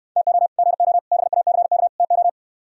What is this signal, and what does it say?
Morse code, heard as a keyed tone, about 68 wpm, 700 Hz. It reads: TO GO BACK TO